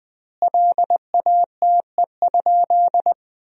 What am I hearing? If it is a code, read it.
Morse code, heard as a keyed tone, about 20 wpm, 700 Hz. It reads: LATE?
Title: 4x Word-Spacing - Sentences from Top 300 Words - 20wpm